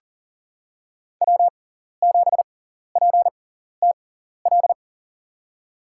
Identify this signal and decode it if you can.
Morse code, heard as a keyed tone, about 38 wpm, spaced as for 14 wpm, 700 Hz. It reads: W7PTL
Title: USA Call Signs - 6x Character Spacing - 40wpm